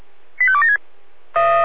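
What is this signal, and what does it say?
Five-tone selective call (ZVEI), heard as quick runs of five tones, tones of about 70 ms. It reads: 85276
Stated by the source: ZVEI five-tone call received on SDR.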